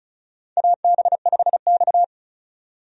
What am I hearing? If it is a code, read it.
Morse code, heard as a keyed tone, about 35 wpm, 700 Hz. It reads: AB5X
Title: USA Call Signs - 4 Characters - 35wpm